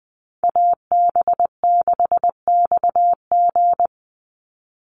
Morse code, heard as a keyed tone, about 20 wpm, 700 Hz. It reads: AB6XG
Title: Call Signs - 20wpm